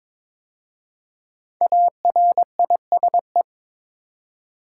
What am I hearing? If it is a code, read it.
Morse code, heard as a keyed tone, about 22 wpm, 700 Hz. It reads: ARISE
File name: 5 Letter Words - First 3 Letters Given - 22wpm